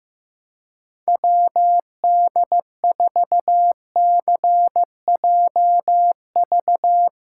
Morse code, heard as a keyed tone, about 15 wpm, 700 Hz. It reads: WD4CJV
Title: USA Callsign DE Callsign - 6 Characters - 15wpm